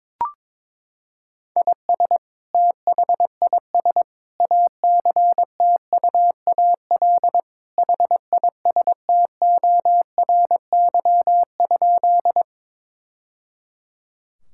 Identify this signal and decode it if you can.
Morse code, heard as a keyed tone, about 22 wpm, 700 Hz. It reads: IS THIS ACTUAL HISTORY?
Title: Sentences from Top 1000 Words - Review Part 5 - 22wpm